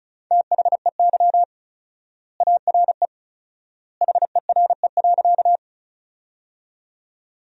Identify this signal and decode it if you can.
Morse code, heard as a keyed tone, about 36 wpm, 700 Hz. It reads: THEY ARE HERE.
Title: 4x Word-Spacing - Sentences from Top 200 Words - 35wpm